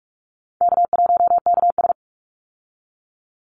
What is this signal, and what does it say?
Morse code, heard as a keyed tone, about 44 wpm, 700 Hz. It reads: K1KS
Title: Easy Call Signs Repeated 3x - 45wpm